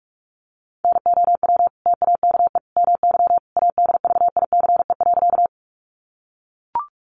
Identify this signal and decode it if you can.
Morse code, heard as a keyed tone, about 44 wpm, 700 Hz. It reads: NOW TAKE MY ADVICE.
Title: Sentences from Top 1100 Words 45wpm